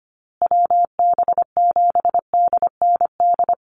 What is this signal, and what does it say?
Morse code, heard as a keyed tone, about 25 wpm, 700 Hz. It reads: WB7DND
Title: Call Signs Repeated 2x - 25wpm